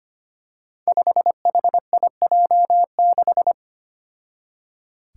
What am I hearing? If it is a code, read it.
Morse code, heard as a keyed tone, about 25 wpm, 700 Hz. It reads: 5HIJ6
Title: Five Challenging Letter-Number Combinations 25wpm